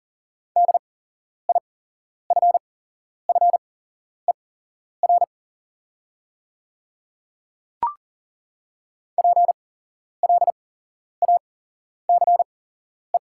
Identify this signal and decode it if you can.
Morse code, heard as a keyed tone, about 41 wpm, spaced as for 11 wpm, 690 Hz. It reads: DIFFER PLACE
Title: 8x Character Spacing - Top 300 Words - 40wpm